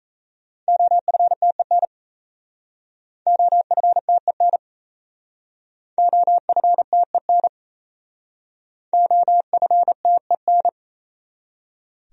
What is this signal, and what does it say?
Morse code, heard as a keyed tone, about 36 wpm, 700 Hz. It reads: OFTEN OFTEN OFTEN OFTEN